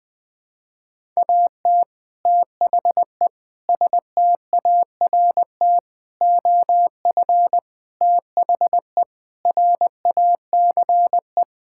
Morse code, heard as a keyed tone, about 20 wpm, 700 Hz. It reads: AT THE START OF THE RACE
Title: Sentences with 5-Word Phrases - 20wpm